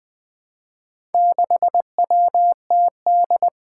Morse code, heard as a keyed tone, about 20 wpm, 700 Hz. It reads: 6WTD
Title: Four Challenging Letter-Number Combinations 20wpm